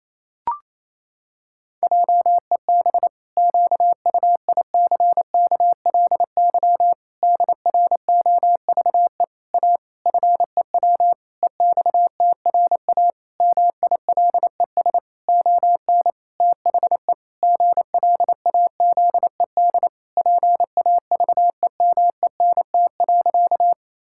Morse code, heard as a keyed tone, about 28 wpm, 700 Hz. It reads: JEB QUICKLY DROVE A FEW EXTRA MILES ON THE GLAZED PAVEMENT.
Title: Pangram Challenge v2 - 28wpm